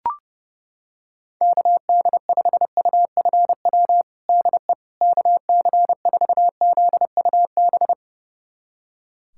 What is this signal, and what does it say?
Morse code, heard as a keyed tone, about 32 wpm, 700 Hz. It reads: KD5UFW DE KC4ZUB
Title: USA Callsign DE Callsign - 6 Characters - 30wpm